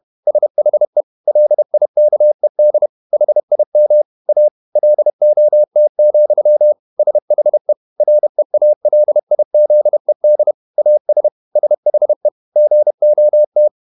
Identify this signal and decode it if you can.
Morse code, heard as a keyed tone, about 31 wpm, 600 Hz. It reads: SHE LIKED HIM A LOT, SHE REALIZED AS SHE GOT
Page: Never 31wpm | CW med Gnister